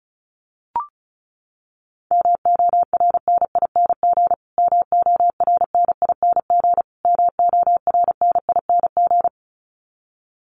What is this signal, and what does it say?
Morse code, heard as a keyed tone, about 34 wpm, 700 Hz. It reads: MORNING MORNING MORNING
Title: Top 500 Words Repeated 35wpm